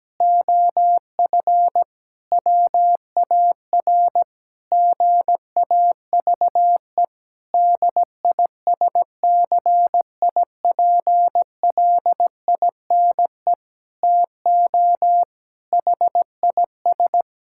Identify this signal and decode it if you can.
Morse code, heard as a keyed tone, about 17 wpm, 700 Hz. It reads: OF WAR GAVE DISCIPLINE TO HIS
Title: Sentences from Top 1000 Words - Review Part 6 - 17wpm